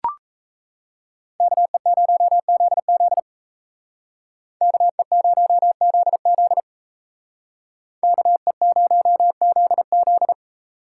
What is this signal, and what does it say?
Morse code, heard as a keyed tone, about 36 wpm, 700 Hz. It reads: KE0ZZ KE0ZZ KE0ZZ